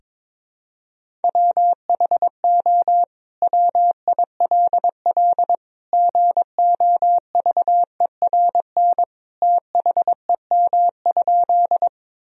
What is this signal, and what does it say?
Morse code, heard as a keyed tone, about 22 wpm, 700 Hz. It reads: WHO WILL GOVERN THEM?
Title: Sentences from Top 1000 Words - Review Part 2 - 22wpm